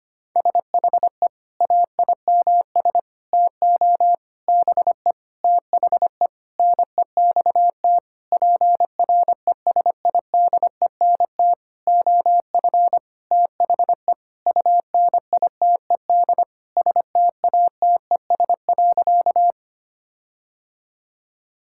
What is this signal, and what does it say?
Morse code, heard as a keyed tone, about 25 wpm, 700 Hz. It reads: SHE AIMS TO BE THE NEXT PRESIDENT OF THE UNITED STATES.
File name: practice / 25wpm / Sentences with 9-Word Phrases - 25wpm